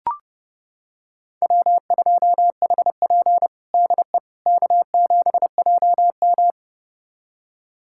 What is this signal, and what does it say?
Morse code, heard as a keyed tone, about 30 wpm, 700 Hz. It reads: W2HP DE K7JM
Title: USA Callsign DE Callsign - 4 Characters - 30wpm